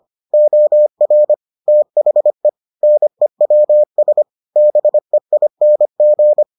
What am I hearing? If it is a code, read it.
Morse code, heard as a keyed tone, about 25 wpm, 600 Hz. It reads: OR THE NEWS BEING